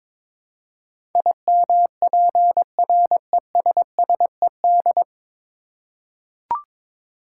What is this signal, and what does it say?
Morse code, heard as a keyed tone, about 22 wpm, 700 Hz. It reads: IMPRESSED
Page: Morse Code Ninja practice 22wpm